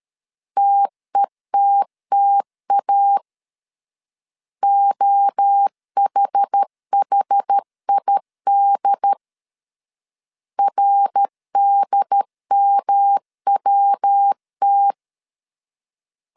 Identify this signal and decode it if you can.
Morse code, heard as a keyed tone, about 13 wpm, 800 Hz. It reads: TETTA OHHID RDMWT
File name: CW / DET / 1 3 1500C S B 12wpm 2828